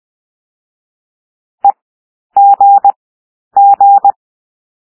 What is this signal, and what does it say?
Morse code, heard as a keyed tone, about 20 wpm, 800 Hz. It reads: E G G
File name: cwmp3 / W1 SOUND WORDS 20 wpm